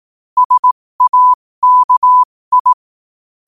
Morse code, heard as a keyed tone, about 18 wpm, spaced as for 15 wpm, 1000 Hz. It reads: SAKI
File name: qotdfiles / 15wpm